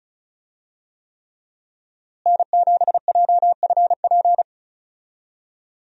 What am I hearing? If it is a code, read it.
Morse code, heard as a keyed tone, about 34 wpm, 700 Hz. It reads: N7JFP
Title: USA Call Signs - 5 Characters - Repeated 3x - 35wpm